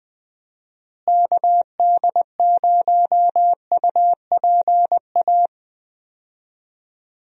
Morse code, heard as a keyed tone, about 20 wpm, 700 Hz. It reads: KD0UPA
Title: USA Call Signs - 6 Characters - Repeated 2x - 20wpm